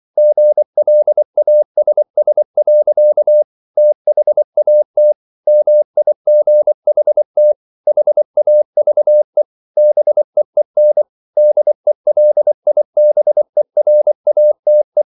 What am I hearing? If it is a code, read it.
Morse code, heard as a keyed tone, about 24 wpm, 600 Hz. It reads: GLASS. THAT MIGHT HAVE BEEN DELIBERATE